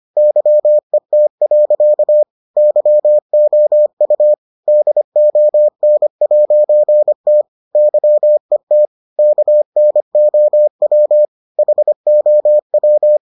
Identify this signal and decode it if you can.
Morse code, heard as a keyed tone, about 25 wpm, 600 Hz. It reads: YET. YOU DON'T YET KNOW HOW